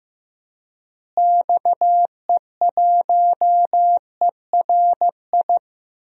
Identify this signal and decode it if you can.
Morse code, heard as a keyed tone, about 16 wpm, 700 Hz. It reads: XE1ERI
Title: DXCC-100 Call Signs - 6 Characters - Repeated 3x - 15wpm